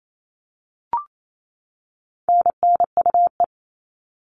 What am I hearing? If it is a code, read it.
Morse code, heard as a keyed tone, about 28 wpm, 700 Hz. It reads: NNUE
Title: All Cut Numbers - 4-Digit 28wpm